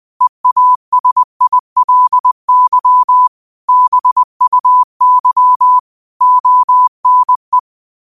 Morse code, heard as a keyed tone, about 20 wpm, 1000 Hz. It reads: EASILY BUY ONE